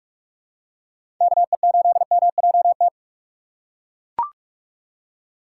Morse code, heard as a keyed tone, about 41 wpm, 700 Hz. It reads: KE8MJT
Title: USA Call Signs - 6 Characters - 2x Character Spacing - 45wpm